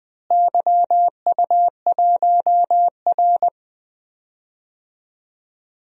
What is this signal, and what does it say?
Morse code, heard as a keyed tone, about 20 wpm, 700 Hz. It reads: YU1R
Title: DXCC-100 Call Signs - 4 Characters - Repeated 2x - 20wpm